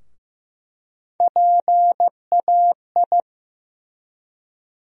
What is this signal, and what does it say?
Morse code, heard as a keyed tone, about 15 wpm, 700 Hz. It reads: PAI
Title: USA Call Sign Postfixes - 3 Letters 15wpm